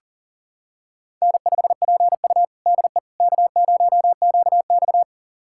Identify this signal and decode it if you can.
Morse code, heard as a keyed tone, about 37 wpm, 700 Hz. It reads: N5PU DE K0QX
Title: USA Callsign DE Callsign - 4 Characters - 40wpm